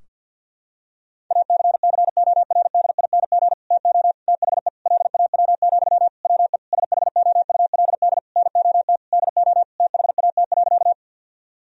Morse code, heard as a keyed tone, about 52 wpm, 700 Hz. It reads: ACCORDING TO THE LAW, WE SHOULD NOT DO THAT.
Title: Sentences from Top 1000 Words - Review Part 4 - 50wpm